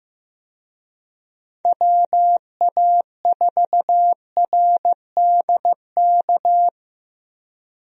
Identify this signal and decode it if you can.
Morse code, heard as a keyed tone, about 15 wpm, 700 Hz. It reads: WA4RDK
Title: USA Call Signs - 6 Characters - Repeated 3x - 15wpm